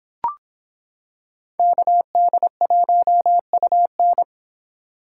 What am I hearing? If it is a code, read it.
Morse code, heard as a keyed tone, about 25 wpm, 700 Hz. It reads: KD1UN